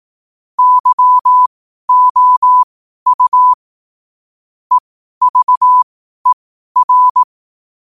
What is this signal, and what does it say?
Morse code, heard as a keyed tone, about 18 wpm, spaced as for 12 wpm, 1000 Hz. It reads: YOU EVER
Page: Quotes for Wed, 13 Aug 2025 in Morse Code at 12 words per minute.